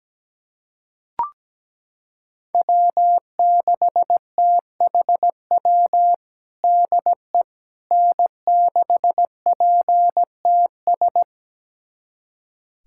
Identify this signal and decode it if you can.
Morse code, heard as a keyed tone, about 17 wpm, 700 Hz. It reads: W6THW DE N6PTS
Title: USA Callsign DE Callsign - 5 Characters - 17wpm